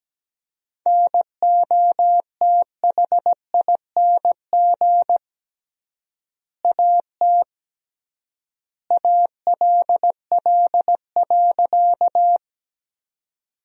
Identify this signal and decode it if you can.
Morse code, heard as a keyed tone, about 17 wpm, 700 Hz. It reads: NOTHING AT ALL.
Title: 3x Word-Spacing - Sentences from Top 500 Words - 17wpm